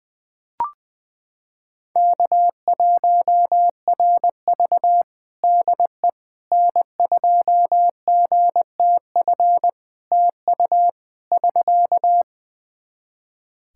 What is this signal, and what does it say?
Morse code, heard as a keyed tone, about 20 wpm, 700 Hz. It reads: K1RV DE N2GTF TU <SK>
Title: Practice QSO - Exchange 4 - Ending - 20wpm